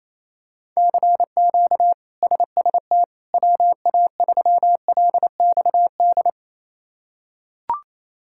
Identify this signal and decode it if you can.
Morse code, heard as a keyed tone, about 28 wpm, 700 Hz. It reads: CQ SST WA3LXD
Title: K1USN SST QSO Practice - 28wpm